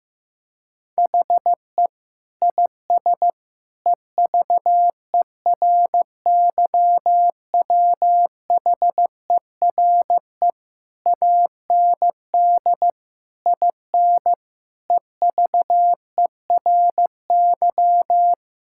Morse code, heard as a keyed tone, about 15 wpm, 700 Hz. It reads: HE IS EVERYWHERE AND IN EVERY
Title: Sentences from Top 1600 Words 15wpm